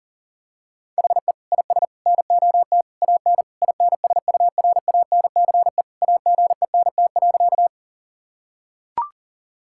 Morse code, heard as a keyed tone, about 40 wpm, 700 Hz. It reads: HE IS NOT AN INSURANCE AGENT.
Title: Sentences from Top 1600 Words 40wpm